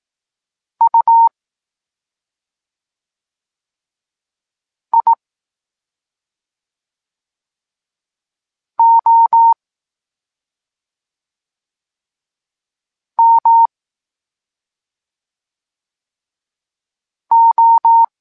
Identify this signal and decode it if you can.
Morse code, heard as a keyed tone, about 18 wpm, spaced as for 2 wpm, 920 Hz. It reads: UIOMO
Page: Tekst 8 – doskonalenie odbioru poznanych liter